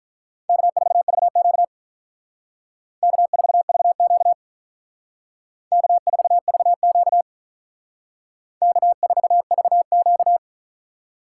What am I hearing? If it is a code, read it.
Morse code, heard as a keyed tone, about 46 wpm, 700 Hz. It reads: K4VQ K4VQ K4VQ K4VQ